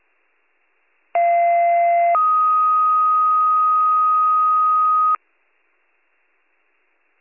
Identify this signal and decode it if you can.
Two-tone page, heard as a one-second tone, then a three-second tone, about 690 then 1250 Hz. It is heard from 1.1 to 5.2 s.